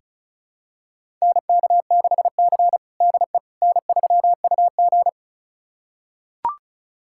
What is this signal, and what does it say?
Morse code, heard as a keyed tone, about 36 wpm, 700 Hz. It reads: NK6C DE N3UG
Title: USA Callsign DE Callsign - 4 Characters - 35wpm